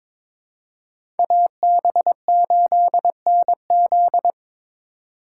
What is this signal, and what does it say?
Morse code, heard as a keyed tone, about 22 wpm, 700 Hz. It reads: AB8NZ
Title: USA Call Signs - 5 Characters - 3x Character Spacing - 22wpm